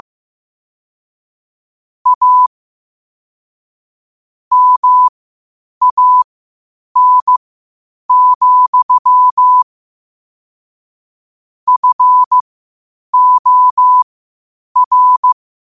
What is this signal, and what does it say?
Morse code, heard as a keyed tone, about 15 wpm, spaced as for 8 wpm, 1000 Hz. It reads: A MAN, FOR
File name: qotdfiles / 8wpm